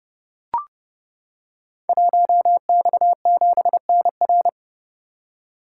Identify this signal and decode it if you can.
Morse code, heard as a keyed tone, about 30 wpm, 700 Hz. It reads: 1X7NR
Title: Code Groups - 5 Letter:Numbers - 30wpm